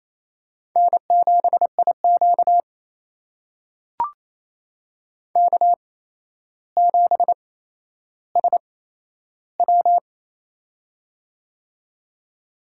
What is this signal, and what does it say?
Morse code, heard as a keyed tone, about 27 wpm, spaced as for 12 wpm, 700 Hz. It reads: N7IQ K7SW
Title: USA Call Signs - 4 Characters - 8x Character Spacing - 28wpm